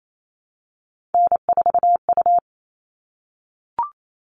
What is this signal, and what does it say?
Morse code, heard as a keyed tone, about 28 wpm, 700 Hz. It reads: N4U
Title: All Cut Numbers - 3-Digit 28wpm